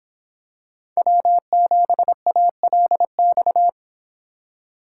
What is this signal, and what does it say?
Morse code, heard as a keyed tone, about 25 wpm, 700 Hz. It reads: W7ALX